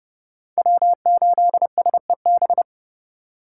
Morse code, heard as a keyed tone, about 29 wpm, 700 Hz. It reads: W8SEB